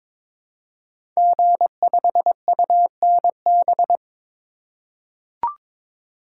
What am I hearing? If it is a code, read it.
Morse code, heard as a keyed tone, about 23 wpm, 700 Hz. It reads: G5UNB
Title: Code Groups - 5 Letter:Numbers - 22wpm